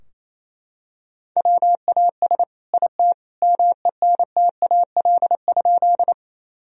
Morse code, heard as a keyed tone, about 28 wpm, 700 Hz. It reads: WAS IT MENTAL?
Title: Sentences from Top 1000 Words - Review Part 4 - 28wpm